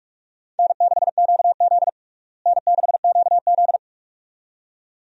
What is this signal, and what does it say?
Morse code, heard as a keyed tone, about 46 wpm, 700 Hz. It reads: N6QZ N6QZ
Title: USA Call Signs - 4 Characters - Repeated 2x - 45wpm